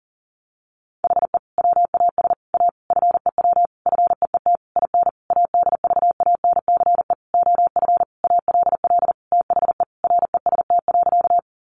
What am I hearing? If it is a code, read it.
Morse code, heard as a keyed tone, about 40 wpm, 700 Hz. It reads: HE WAS A FEW FEET IN ADVANCE OF ALL THE REST.
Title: Sentences from Top 1400 Words 40wpm